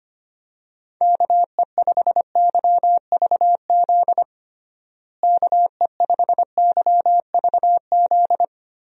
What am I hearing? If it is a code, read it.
Morse code, heard as a keyed tone, about 25 wpm, 700 Hz. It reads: KE5YVZ KE5YVZ
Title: USA Call Signs - 6 Characters - Repeated 3x - 25wpm